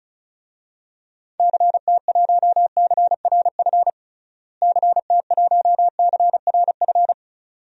Morse code, heard as a keyed tone, about 36 wpm, 700 Hz. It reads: CT1CRF CT1CRF
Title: DXCC-100 Call Signs - 6 Characters - Repeated 2x - 35wpm